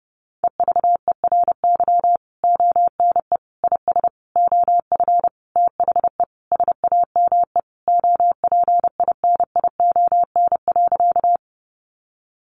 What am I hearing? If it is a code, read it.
Morse code, heard as a keyed tone, about 28 wpm, 700 Hz. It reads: EVERY ONE IS OF THE SAME OPINION.
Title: Sentences from Top 1200 Words 30wpm